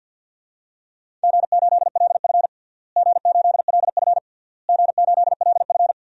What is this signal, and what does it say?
Morse code, heard as a keyed tone, about 50 wpm, 700 Hz. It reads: G8LF G8LF G8LF
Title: DXCC-100 Call Signs - 4 Characters - Repeated 3x - 50wpm